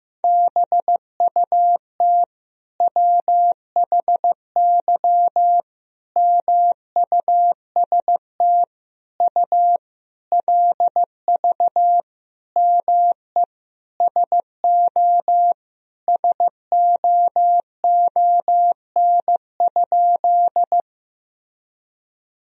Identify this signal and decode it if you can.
Morse code, heard as a keyed tone, about 15 wpm, 700 Hz. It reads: BUT WHY MUST U LV ME SO SOON?
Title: Sentences from Top 400 Words and Abbreviations 15wpm